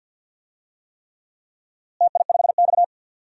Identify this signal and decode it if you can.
Morse code, heard as a keyed tone, about 46 wpm, 700 Hz. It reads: TI5X
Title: DXCC-100 Call Signs - 4 Characters - Repeated 3x - 50wpm